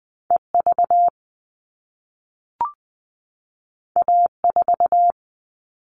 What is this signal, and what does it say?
Morse code, heard as a keyed tone, about 20 wpm, 700 Hz. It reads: EV A4
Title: All Cut Numbers - 2-Digit 20wpm